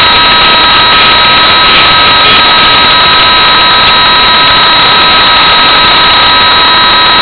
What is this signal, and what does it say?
PSK31, about 31.25 baud.